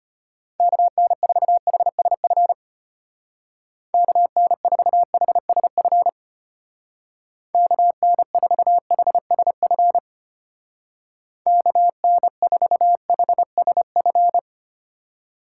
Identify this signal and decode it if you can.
Morse code, heard as a keyed tone, about 30 wpm, 700 Hz. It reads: KN4HSF KN4HSF KN4HSF KN4HSF
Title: Speed Racing - USA 2x3 Call Signs 25wpm